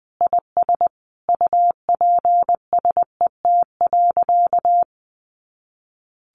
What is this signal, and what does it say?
Morse code, heard as a keyed tone, about 20 wpm, 700 Hz. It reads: IS UPSET.
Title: Sentences from Top 800 Words 20wpm